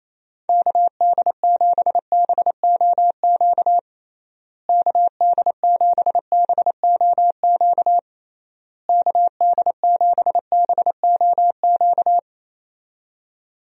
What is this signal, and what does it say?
Morse code, heard as a keyed tone, about 27 wpm, 700 Hz. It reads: KD7BOQ KD7BOQ KD7BOQ
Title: USA Call Signs - 6 Characters - Repeated 3x - 28wpm